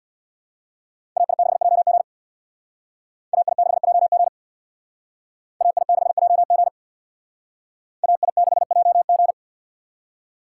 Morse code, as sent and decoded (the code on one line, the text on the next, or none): .- .. -.... .--- --. / .- .. -.... .--- --. / .- .. -.... .--- --. / .- .. -.... .--- --.
AI6JG AI6JG AI6JG AI6JG